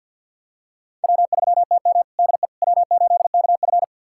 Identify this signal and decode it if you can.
Morse code, heard as a keyed tone, about 51 wpm, 700 Hz. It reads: W3TM DE W8KF
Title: USA Callsign DE Callsign - 4 Characters - 50wpm